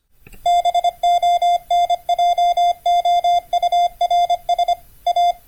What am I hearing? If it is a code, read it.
Morse code, heard as a keyed tone, about 25 wpm, 700 Hz. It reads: BONJOURS A